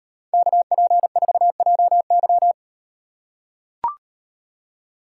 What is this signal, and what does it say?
Morse code, heard as a keyed tone, about 36 wpm, 700 Hz. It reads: KP4JY